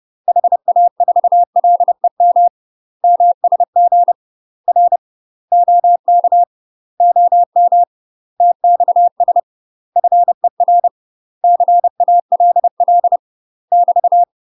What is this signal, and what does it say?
Morse code, heard as a keyed tone, about 30 wpm, 700 Hz. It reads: HA4LEM MSG R OK OM TXS FER CALL <BT>
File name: UK MASTER QSOs TRAINING/30WPM